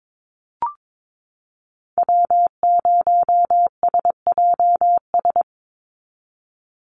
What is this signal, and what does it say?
Morse code, heard as a keyed tone, about 22 wpm, 700 Hz. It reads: W0SJS
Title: Call Signs Repeated 2x - 22wpm